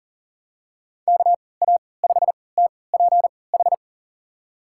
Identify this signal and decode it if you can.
Morse code, heard as a keyed tone, about 40 wpm, spaced as for 23 wpm, 700 Hz. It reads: KA5TPH
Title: USA Call Signs - 6 Characters - 3x Character Spacing - 40wpm